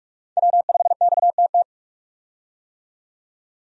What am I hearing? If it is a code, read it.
Morse code, heard as a keyed tone, about 41 wpm, 700 Hz. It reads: W5XTT
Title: USA Call Signs - 5 Characters - 45wpm